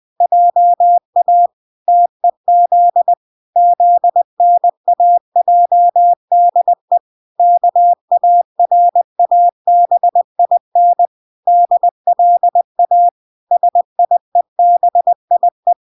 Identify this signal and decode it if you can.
Morse code, heard as a keyed tone, about 20 wpm, 700 Hz. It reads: JA TEZ ZNAJDE KARABIN DLA SIEBIE